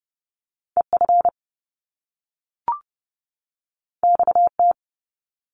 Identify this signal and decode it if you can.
Morse code, heard as a keyed tone, about 30 wpm, 700 Hz. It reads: EF XT
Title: Top Two Letter Combinations in Words 30wpm